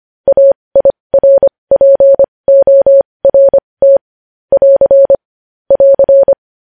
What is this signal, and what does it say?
Morse code, heard as a keyed tone, about 25 wpm, spaced as for 20 wpm, 550 Hz. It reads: AIRPORT <AR> <AR>
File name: podcast / News headlines-1x-25x20